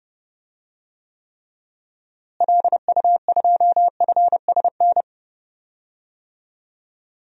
Morse code, heard as a keyed tone, about 30 wpm, 700 Hz. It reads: LU2FSN
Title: DXCC-100 Call Signs - 6 Characters - Repeated 3x - 30wpm